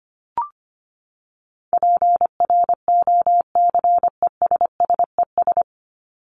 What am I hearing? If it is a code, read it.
Morse code, heard as a keyed tone, about 25 wpm, 700 Hz. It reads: PROCESSES
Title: ES:S Suffix - Top 100 Words with Base Words 25wpm